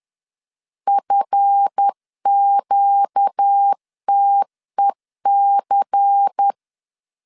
Morse code, heard as a keyed tone, about 10 wpm, 800 Hz. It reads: FQTEC
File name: LEZIONI / 8 YJ45 / 8 YJ 10wpm 2651